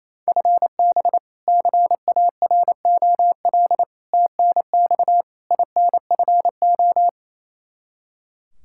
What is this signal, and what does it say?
Morse code, heard as a keyed tone, about 27 wpm, 700 Hz. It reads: FB CAROL TNX INFO
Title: Practice QSO - Exchange 3 - More Info - 28wpm